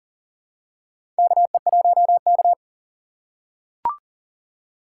Morse code, heard as a keyed tone, about 40 wpm, 690 Hz. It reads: KE1K
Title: USA Call Signs - 4 Characters - 2x Character Spacing - 40wpm